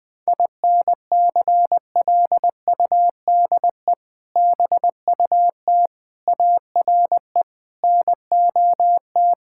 Morse code, heard as a keyed tone, about 20 wpm, 700 Hz. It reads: INCLUDE BUT ARE NOT
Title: Sets of 6 Words - 20wpm